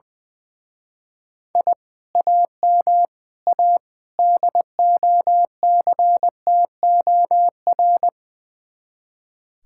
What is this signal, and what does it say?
Morse code, heard as a keyed tone, about 20 wpm, 700 Hz. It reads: I AM A DOCTOR
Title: Speed-Racing Common QSO Phrases Encore - 20wpm